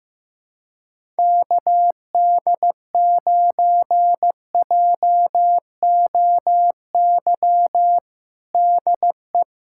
Morse code, heard as a keyed tone, about 15 wpm, 700 Hz. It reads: KD9JOY DE
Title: USA Callsign DE Callsign - 6 Characters - 15wpm